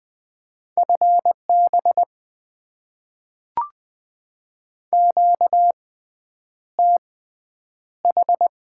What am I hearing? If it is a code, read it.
Morse code, heard as a keyed tone, about 20 wpm, spaced as for 9 wpm, 690 Hz. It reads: FB QTH
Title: Top 20 QSO Elements - 6x Character Spacing - 20wpm